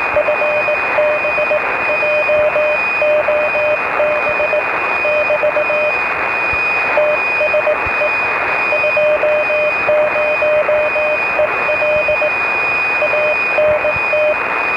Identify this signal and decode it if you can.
Morse code, heard as a keyed tone, about 19 wpm, 620 Hz, with radio noise. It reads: FB JOB <BT> TSE 20EL ANT